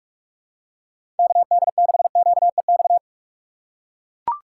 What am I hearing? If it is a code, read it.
Morse code, heard as a keyed tone, about 41 wpm, 700 Hz. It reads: KD6QEX
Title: USA Call Signs - 6 Characters - 4x Character Spacing - 45wpm